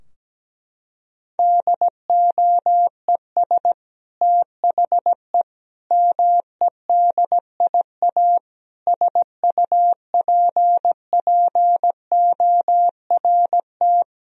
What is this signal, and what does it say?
Morse code, heard as a keyed tone, about 17 wpm, 700 Hz. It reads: DOES THE MEDIA SUPPORT
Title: Sentences from Top 1000 Words - Review Part 4 - 17wpm